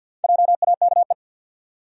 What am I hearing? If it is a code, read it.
Morse code, heard as a keyed tone, about 47 wpm, 700 Hz. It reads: JAKE
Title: USA Common Names - Max 4 Letters - 50wpm